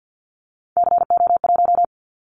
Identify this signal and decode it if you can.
Morse code, heard as a keyed tone, about 50 wpm, 700 Hz. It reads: CO1
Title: Callsign Prefixes for Top 100 DXCC - 50wpm